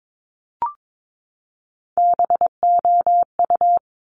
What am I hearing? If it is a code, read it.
Morse code, heard as a keyed tone, about 22 wpm, 700 Hz. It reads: BOU
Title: Top Three Letter Combinations in Words 22wpm